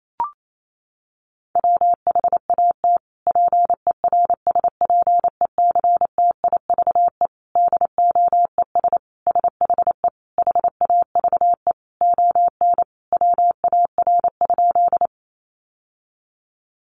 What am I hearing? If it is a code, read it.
Morse code, heard as a keyed tone, about 28 wpm, 700 Hz. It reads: WHAT PERSPECTIVE DOES SHE HAVE ON WAR?
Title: Sentences from Top 1200 Words 28wpm